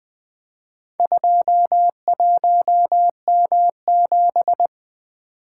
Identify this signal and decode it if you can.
Morse code, heard as a keyed tone, about 21 wpm, 700 Hz. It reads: 21M7
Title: Four Letter-Number Combinations - 20wpm